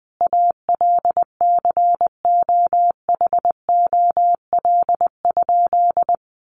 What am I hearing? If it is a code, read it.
Morse code, heard as a keyed tone, about 20 wpm, 700 Hz. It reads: ALCOHOL?
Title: Sentences from Top 1400 Words 20wpm